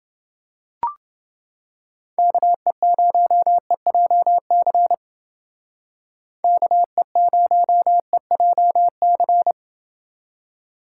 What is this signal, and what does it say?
Morse code, heard as a keyed tone, about 29 wpm, 700 Hz. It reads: KE0EJC KE0EJC